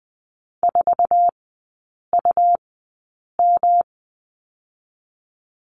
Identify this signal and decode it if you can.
Morse code, heard as a keyed tone, about 20 wpm, spaced as for 8 wpm, 690 Hz. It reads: 4UM
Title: 145 Lesson 24e - Letter y - Three Characters 20wpm